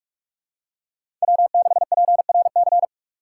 Morse code, heard as a keyed tone, about 46 wpm, 700 Hz. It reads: W6PRC
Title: USA Call Signs - 5 Characters - 4x Character Spacing - 45wpm